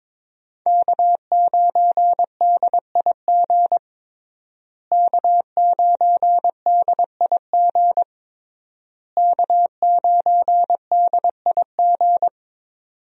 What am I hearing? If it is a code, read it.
Morse code, heard as a keyed tone, about 22 wpm, 700 Hz. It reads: K9DIG K9DIG K9DIG
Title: USA Call Signs - 5 Characters - Repeated 3x - 22wpm